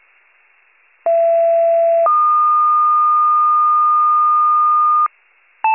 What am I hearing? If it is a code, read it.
Two-tone page, heard as a one-second tone, then a three-second tone, about 670 then 1220 Hz. It is heard from 1.1 to 5.1 s.